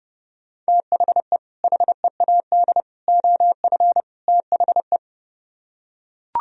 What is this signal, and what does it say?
Morse code, heard as a keyed tone, about 30 wpm, 700 Hz. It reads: THE HEAD OF THE